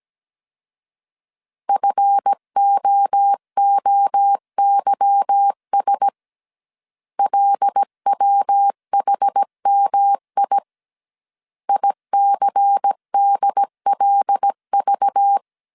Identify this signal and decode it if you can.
Morse code, heard as a keyed tone, about 17 wpm, 800 Hz. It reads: FOOYS LWHMI ICDLV
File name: CW / DET / 1 9 1500C S A 16wpm 2828